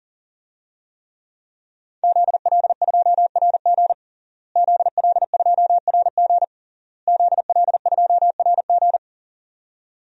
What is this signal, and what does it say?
Morse code, heard as a keyed tone, about 41 wpm, 700 Hz. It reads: ZL2RG ZL2RG ZL2RG
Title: DXCC-100 Call Signs - 5 Characters - Repeated 3x - 40wpm